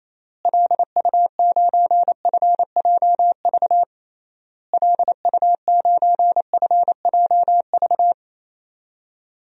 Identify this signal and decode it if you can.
Morse code, heard as a keyed tone, about 28 wpm, 700 Hz. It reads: LU9FJV LU9FJV
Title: DXCC-100 Call Signs - 6 Characters - Repeated 3x - 28wpm